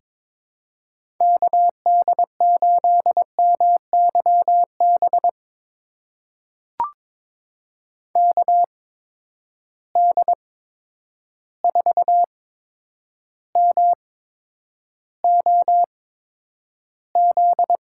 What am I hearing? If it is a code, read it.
Morse code, heard as a keyed tone, about 22 wpm, spaced as for 9 wpm, 700 Hz. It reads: KD8MYB KD4MOZ
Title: USA Call Signs - 6 Characters - 8x Character Spacing - 22wpm